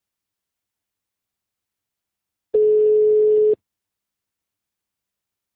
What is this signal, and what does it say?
Ringback tone (Europe, 425 Hz).
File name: out- -2014- - - .32367